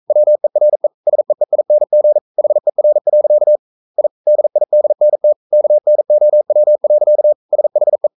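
Morse code, heard as a keyed tone, about 42 wpm, 600 Hz. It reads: WERE SEEING HER. I DIDNT KNOW. SHE
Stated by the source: Morse code sending of an English novel at 42 wpm.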